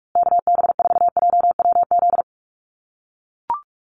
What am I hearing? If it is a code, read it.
Morse code, heard as a keyed tone, about 44 wpm, 700 Hz. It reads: KB4JWZ
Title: USA Call Signs Repeated 2x 45wpm